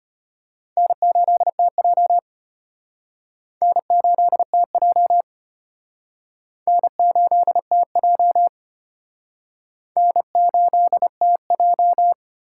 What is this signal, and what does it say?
Morse code, heard as a keyed tone, about 32 wpm, 700 Hz. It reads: N8TJ N8TJ N8TJ N8TJ